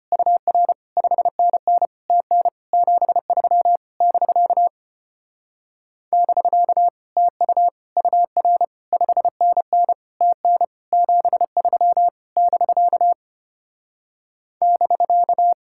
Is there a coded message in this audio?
Yes: Morse code from 0.1 to 15.5 s, heard as a keyed tone, about 33 wpm, 700 Hz. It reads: UR 5NN TN 73 <BK> <BK> TU UR 5NN TN 73 <BK> <BK>